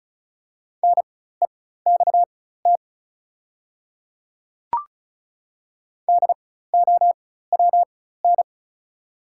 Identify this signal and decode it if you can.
Morse code, heard as a keyed tone, about 36 wpm, spaced as for 16 wpm, 690 Hz. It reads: NEXT DOWN
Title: 4x Character Spacing - Top 300 Words - 35wpm